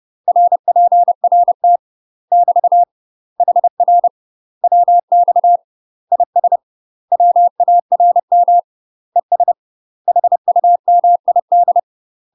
Morse code, heard as a keyed tone, about 30 wpm, 700 Hz. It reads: RPRT <BT> HR WX IS WARM ES HUMID
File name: UK MASTER QSOs TRAINING / 30WPM / UK MASTER QSOs 0051